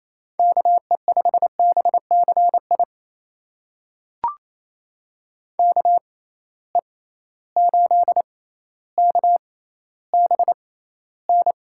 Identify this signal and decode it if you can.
Morse code, heard as a keyed tone, about 28 wpm, spaced as for 14 wpm, 700 Hz. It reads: KE5BCI KE8KBN
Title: USA Call Signs - 6 Characters - 6x Character Spacing - 28wpm